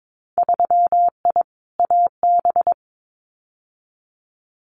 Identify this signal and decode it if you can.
Morse code, heard as a keyed tone, about 22 wpm, 700 Hz. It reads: 3I AB
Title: Winter Field Day Exchanges Encore 22wpm